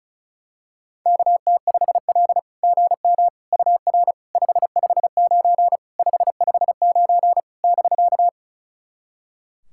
Morse code, heard as a keyed tone, about 36 wpm, 690 Hz. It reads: KT5L GM UR 559 559 <BK>
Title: SOTA QSO Practice 35wpm